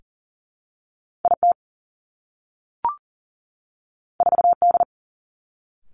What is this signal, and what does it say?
Morse code, heard as a keyed tone, about 42 wpm, 700 Hz. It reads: I T 4 D
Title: Two Challenging Letter-Number Combinations 40wpm